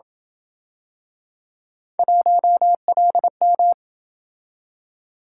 Morse code, heard as a keyed tone, about 27 wpm, 700 Hz. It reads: 1LM